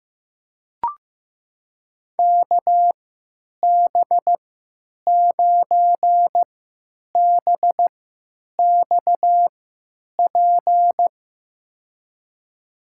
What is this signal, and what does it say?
Morse code, heard as a keyed tone, about 15 wpm, spaced as for 9 wpm, 700 Hz. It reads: KB9BXP
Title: USA Call Signs - 3x Character Spacing - 15wpm